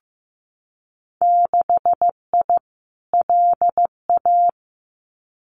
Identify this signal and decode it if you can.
Morse code, heard as a keyed tone, about 15 wpm, 700 Hz. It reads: 6I LA
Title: Winter Field Day Exchanges Encore 15wpm